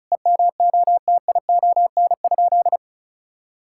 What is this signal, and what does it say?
Morse code, heard as a keyed tone, about 34 wpm, 700 Hz. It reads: EMOTION?
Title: Sentences from Top 1600 Words 35wpm